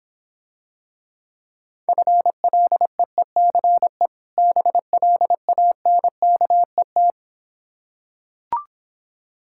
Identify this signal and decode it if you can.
Morse code, heard as a keyed tone, about 26 wpm, 690 Hz. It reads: FLEECE BLANKET